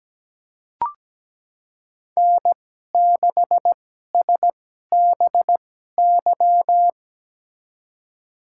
Morse code, heard as a keyed tone, about 17 wpm, spaced as for 12 wpm, 700 Hz. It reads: N6SBY
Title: USA Call Signs - 2x Character Spacing - 17wpm